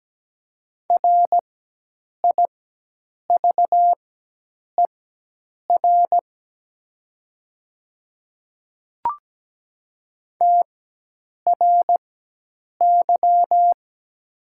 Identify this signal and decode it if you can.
Morse code, heard as a keyed tone, about 17 wpm, spaced as for 8 wpm, 690 Hz. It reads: RIVER TRY
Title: 4x Character Spacing - Top 300 Words - 17wpm